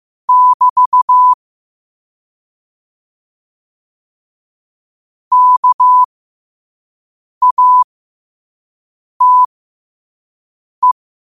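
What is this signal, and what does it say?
Morse code, heard as a keyed tone, about 15 wpm, spaced as for 5 wpm, 1000 Hz. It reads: <BT> KATE
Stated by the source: Morse code practice at 5 wpm.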